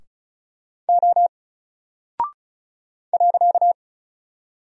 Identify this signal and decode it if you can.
Morse code, heard as a keyed tone, about 36 wpm, 700 Hz. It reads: O .
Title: Single Character (Letter:Number:Punctuation:Prosign) Alt 35wpm